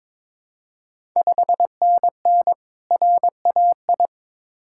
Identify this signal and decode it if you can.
Morse code, heard as a keyed tone, about 21 wpm, 700 Hz. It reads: 5NN RAI
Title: Ontario QSO Party Exchange 22wpm